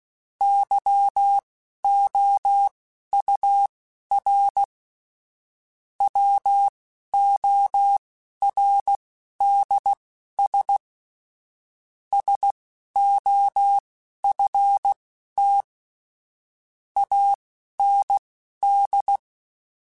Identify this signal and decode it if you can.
Morse code, heard as a keyed tone, about 16 wpm, spaced as for 11 wpm, 800 Hz. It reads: YOUR WORDS SOFT AND
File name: Chuckles / Chuckles-mp3s / SoftWords @10